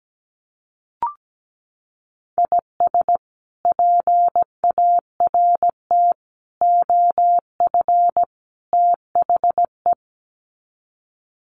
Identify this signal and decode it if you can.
Morse code, heard as a keyed tone, about 17 wpm, 700 Hz. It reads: IS PART OF THE
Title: Sets of 4 Words 17wpm